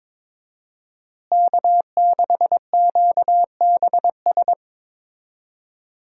Morse code, heard as a keyed tone, about 21 wpm, 700 Hz. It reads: K6QBS
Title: USA Call Signs - 5 Characters - Repeated 2x - 22wpm